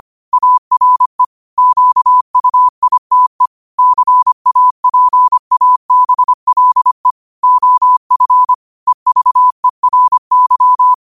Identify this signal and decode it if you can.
Morse code, heard as a keyed tone, about 25 wpm, 1000 Hz. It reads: ARE QUITE CAPABLE OF EVERY
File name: qotdfiles / 25wpm